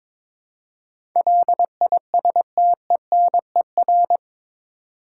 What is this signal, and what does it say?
Morse code, heard as a keyed tone, about 21 wpm, 700 Hz. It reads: LISTENER
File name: practice / 22wpm / Eight Letter Words - 22wpm